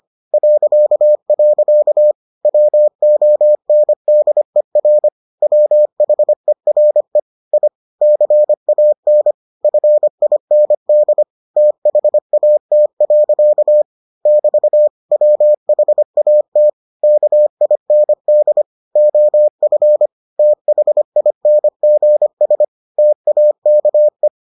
Morse code, heard as a keyed tone, about 25 wpm, 600 Hz. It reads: .. WONDER WHERE I CAN FIND THAT. <BT> WHAT KIND OF THINGS TAKE